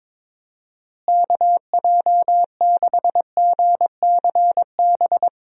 Morse code, heard as a keyed tone, about 22 wpm, 700 Hz. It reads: KJ6GCB
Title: USA Call Signs - 6 Characters - Repeated 2x - 22wpm